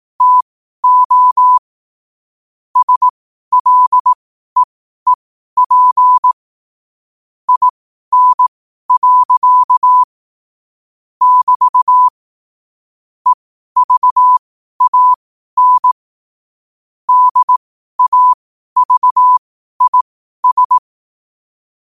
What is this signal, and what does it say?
Morse code, heard as a keyed tone, about 18 wpm, spaced as for 12 wpm, 1000 Hz. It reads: TO SLEEP IN. <BT> EVAN DAVIS
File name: qotdfiles / 12wpm